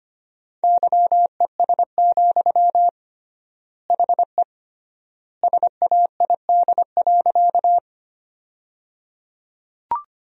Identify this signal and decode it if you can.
Morse code, heard as a keyed tone, about 25 wpm, 700 Hz. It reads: YES, HE SAID.
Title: 3x Word-Spacing - Sentences from Top 500 Words - 25wpm